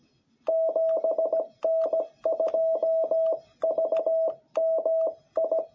Morse code, heard as a keyed tone, about 17 wpm, 660 Hz. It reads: 7N2VMI
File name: jh8cbh / 2021cw